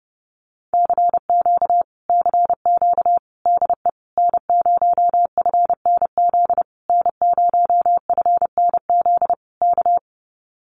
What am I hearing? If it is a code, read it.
Morse code, heard as a keyed tone, about 28 wpm, 700 Hz. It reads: CQ CQ DE N0FNZ N0FNZ K
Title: Common QSO Phrases Encore 30wpm